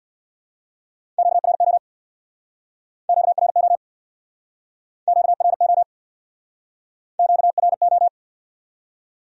Morse code, heard as a keyed tone, about 62 wpm, 700 Hz. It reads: QRO QRO QRO QRO
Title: Speed-Racing Q-Codes - 50wpm